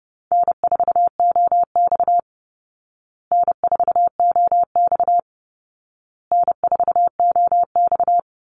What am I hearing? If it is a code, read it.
Morse code, heard as a keyed tone, about 30 wpm, 700 Hz. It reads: N4OX N4OX N4OX
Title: Easy Call Signs Repeated 3x - 30wpm